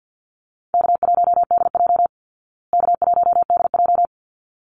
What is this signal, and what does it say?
Morse code, heard as a keyed tone, about 48 wpm, 700 Hz. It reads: K1DJ K1DJ
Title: Call Signs Repeated 3x - 50wpm